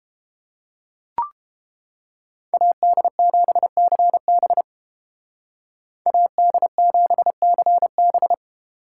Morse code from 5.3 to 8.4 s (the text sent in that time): AD7CB